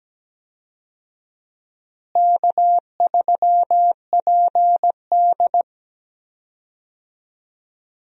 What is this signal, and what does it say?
Morse code, heard as a keyed tone, about 17 wpm, 700 Hz. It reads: K3PD
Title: USA Call Signs - 4 Characters - Repeated 3x - 17wpm